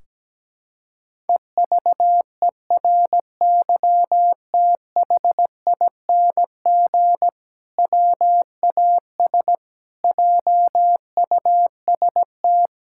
Morse code, heard as a keyed tone, about 17 wpm, 700 Hz. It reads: EVERYTHING WAS JUST
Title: Sentences from Top 1600 Words 17wpm